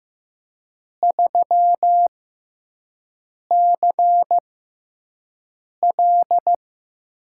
Morse code, heard as a keyed tone, about 15 wpm, spaced as for 5 wpm, 700 Hz. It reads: 3CL
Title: USA Call Signs - 4 Characters - 6x Character Spacing - 15wpm